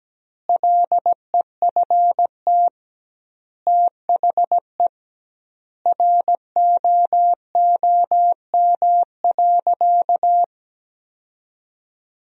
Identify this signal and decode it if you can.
Morse code, heard as a keyed tone, about 17 wpm, 700 Hz. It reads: LEFT THE ROOM.
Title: 2x Word-Spacing - Sentences from Top 300 Words - 17wpm